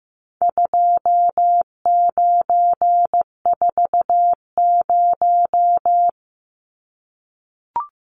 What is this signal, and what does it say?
Morse code, heard as a keyed tone, about 15 wpm, 700 Hz. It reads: 2940
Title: Numbers - 4-Digit 15wpm